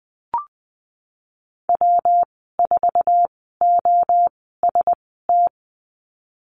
Morse code, heard as a keyed tone, about 20 wpm, spaced as for 15 wpm, 700 Hz. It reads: W4OST
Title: 200 Lesson 32g - Letter k - Call Signs 20wpm